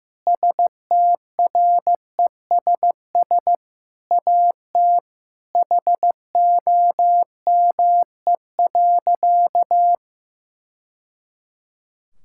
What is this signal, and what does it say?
Morse code, heard as a keyed tone, about 15 wpm, 700 Hz. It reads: STRESS AT HOME.
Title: Sentences from Top 1000 Words - Review Part 6 - 15wpm